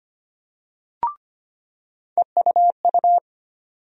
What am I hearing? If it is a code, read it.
Morse code, heard as a keyed tone, about 25 wpm, 700 Hz. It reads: EUU
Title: USA Call Sign Postfixes - 3 Letters 25wpm